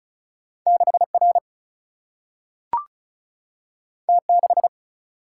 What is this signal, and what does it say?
Morse code, heard as a keyed tone, about 36 wpm, 700 Hz. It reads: 6R T6
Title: Single-Letter Single-Number 35wpm